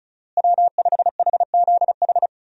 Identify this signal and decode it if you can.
Morse code, heard as a keyed tone, about 36 wpm, 700 Hz. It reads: W5HZH
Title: USA Call Signs - 5 Characters - Repeated 3x - 35wpm